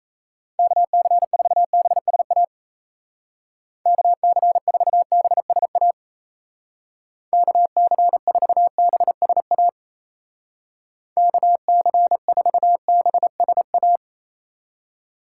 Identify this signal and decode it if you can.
Morse code, heard as a keyed tone, about 35 wpm, 700 Hz. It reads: KC4BSA KC4BSA KC4BSA KC4BSA
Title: Speed Racing - USA Call Signs 28wpm